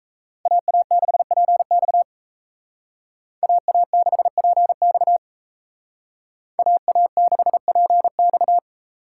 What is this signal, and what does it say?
Morse code, heard as a keyed tone, about 33 wpm, 700 Hz. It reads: AA6PX AA6PX AA6PX
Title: Speed Racing - USA Call Signs 28wpm